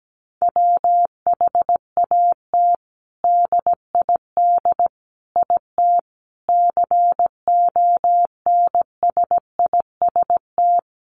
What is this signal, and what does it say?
Morse code, heard as a keyed tone, about 17 wpm, 700 Hz. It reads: WHAT DID IT CONSIST
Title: Sentences from Top 1400 Words 17wpm